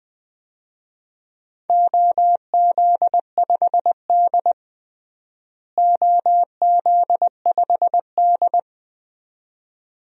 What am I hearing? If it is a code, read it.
Morse code, heard as a keyed tone, about 21 wpm, 700 Hz. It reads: OZ5D OZ5D
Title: DXCC-100 Call Signs - 4 Characters - Repeated 2x - 20wpm